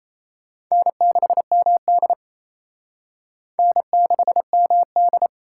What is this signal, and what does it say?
Morse code, heard as a keyed tone, about 37 wpm, 700 Hz. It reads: N6MD N6MD